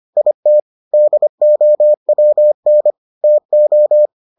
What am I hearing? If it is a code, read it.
Morse code, heard as a keyed tone, about 25 wpm, 600 Hz. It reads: IT DOWN TO